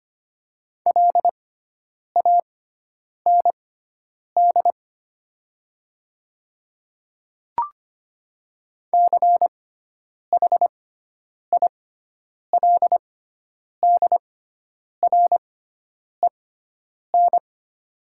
Morse code, heard as a keyed tone, about 25 wpm, spaced as for 9 wpm, 700 Hz. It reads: LAND CHILDREN
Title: 6x Character Spacing - Top 300 Words - 25wpm